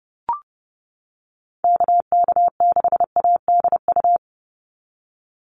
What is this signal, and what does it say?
Morse code, heard as a keyed tone, about 30 wpm, 700 Hz. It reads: KK6ADU
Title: USA Call Signs 30wpm